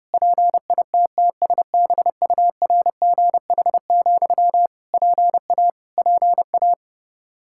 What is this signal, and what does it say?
Morse code, heard as a keyed tone, about 30 wpm, 700 Hz. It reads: PITTSBURGH, PA PA
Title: Practice QSO - Exchange 1 - Beginning - 30wpm